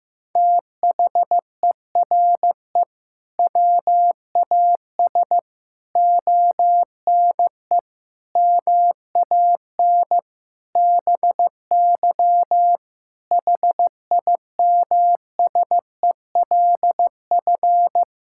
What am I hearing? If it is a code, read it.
Morse code, heard as a keyed tone, about 16 wpm, 700 Hz. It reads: THERE WAS ONE MAN BY HIMSELF
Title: Sentences from Top 1600 Words 15wpm